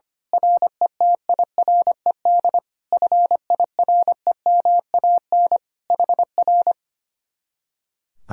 Morse code, heard as a keyed tone, about 24 wpm, 700 Hz. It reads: RETIRED FIREMAN HR
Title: Speed-Racing Common QSO Phrases Encore - 25wpm